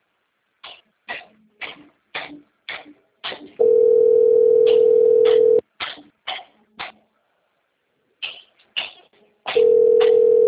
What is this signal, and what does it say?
Ringback tone (North America), 440 + 480 Hz.